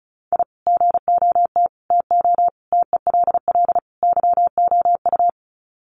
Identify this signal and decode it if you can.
Morse code, heard as a keyed tone, about 34 wpm, 700 Hz. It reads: I GOT TO TELL YOU
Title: Sets of 5 Words Encore2 35wpm